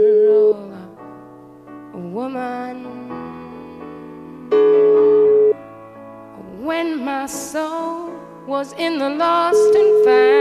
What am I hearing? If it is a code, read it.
Ringback tone (Europe, 425 Hz).